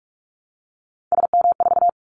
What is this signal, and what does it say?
Morse code, heard as a keyed tone, about 46 wpm, 700 Hz. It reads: SM4
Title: Callsign Prefixes for Top 25 DXCC - 45wpm